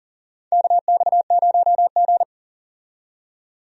Morse code, heard as a keyed tone, about 40 wpm, 700 Hz. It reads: KX0G
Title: USA Call Signs - 4 Characters - 40wpm